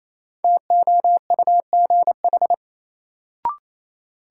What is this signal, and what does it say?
Morse code, heard as a keyed tone, about 28 wpm, 700 Hz. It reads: TOUGH
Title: Sentences from Top 1200 Words - New Word Drill Alt - 28wpm